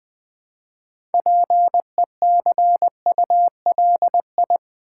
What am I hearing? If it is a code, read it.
Morse code, heard as a keyed tone, about 20 wpm, 700 Hz. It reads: PECULI
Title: Build A Word - 3 Letters At A Time - 8-Letter Words - 20wpm